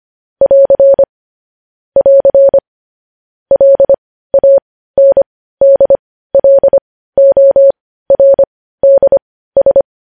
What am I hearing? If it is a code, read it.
Morse code, heard as a keyed tone, about 25 wpm, spaced as for 15 wpm, 550 Hz. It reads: <AR> <AR> LANDLORDS
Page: news-headlines-in-morse-code-at-15-wpm